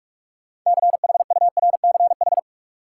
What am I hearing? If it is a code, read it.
Morse code, heard as a keyed tone, about 45 wpm, 700 Hz. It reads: CHURCH
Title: Sets of 5 Words - New Word Drill - 45wpm